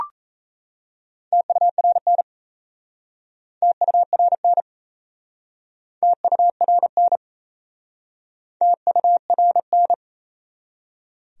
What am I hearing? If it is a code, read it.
Morse code, heard as a keyed tone, about 37 wpm, 700 Hz. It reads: TURN TURN TURN TURN